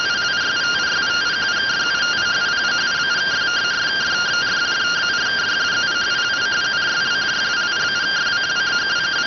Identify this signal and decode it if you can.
Two-tone teleprinter signal, about 170 Hz shift.